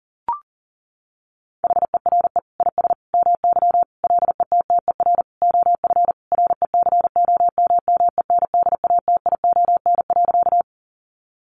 Morse code, heard as a keyed tone, about 37 wpm, 700 Hz. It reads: HERE IS MY LETTER OF RECOMMENDATION.
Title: Sentences from Top 1400 Words 40wpm